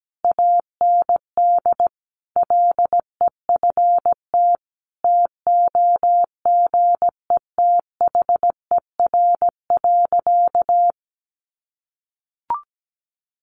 Morse code, heard as a keyed tone, about 17 wpm, 700 Hz. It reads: AND LEFT TOGETHER.
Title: Sentences from Top 400 Words 17wpm